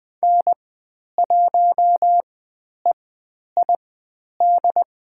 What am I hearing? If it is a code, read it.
Morse code, heard as a keyed tone, about 20 wpm, spaced as for 10 wpm, 700 Hz. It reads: N1EID
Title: 109 Lesson 18g - Letter m - Call Signs 20wpm